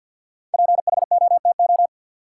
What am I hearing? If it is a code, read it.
Morse code, heard as a keyed tone, about 53 wpm, 700 Hz. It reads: PHOTO
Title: Sentences from Top 1200 Words - New Word Drill Alt - 50wpm